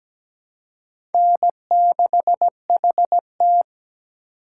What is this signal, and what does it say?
Morse code, heard as a keyed tone, about 20 wpm, 700 Hz. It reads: N6HT
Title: Speed Racing - USA 1x2 Call Signs 17wpm